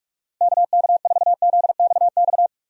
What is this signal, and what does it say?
Morse code, heard as a keyed tone, about 43 wpm, 700 Hz. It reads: KK4ZXX
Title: USA Call Signs - 6 Characters - 45wpm